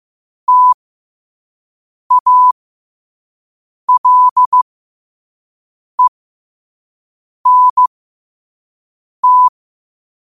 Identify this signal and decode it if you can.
Morse code, heard as a keyed tone, about 15 wpm, spaced as for 5 wpm, 1000 Hz. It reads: TALENT
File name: qotdfiles / 5wpm